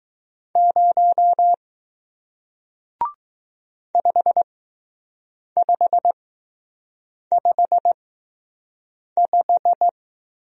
Morse code, heard as a keyed tone, about 20 wpm, spaced as for 6 wpm, 700 Hz. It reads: 0 5555